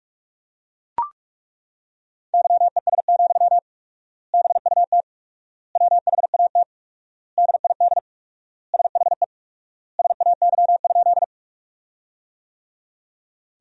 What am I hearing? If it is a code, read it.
Morse code, heard as a keyed tone, about 42 wpm, 700 Hz. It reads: YES, BUT WHAT DID SHE SAY?
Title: 4x Word-Spacing - Sentences from Top 500 Words - 45wpm